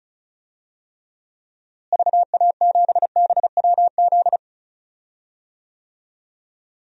Morse code, heard as a keyed tone, about 36 wpm, 700 Hz. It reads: VA7BWZ
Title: DXCC-100 Call Signs - 6 Characters - Repeated 2x - 35wpm